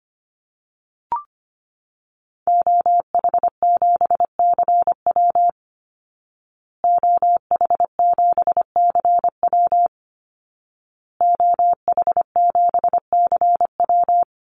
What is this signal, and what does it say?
Morse code, heard as a keyed tone, about 25 wpm, 700 Hz. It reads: OH7CW OH7CW OH7CW
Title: Call Signs Repeated 2x - 25wpm